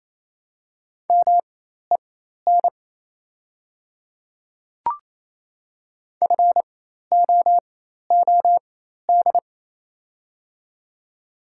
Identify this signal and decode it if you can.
Morse code, heard as a keyed tone, about 28 wpm, spaced as for 13 wpm, 690 Hz. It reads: MEN FOOD
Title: 4x Character Spacing - Top 300 Words - 28wpm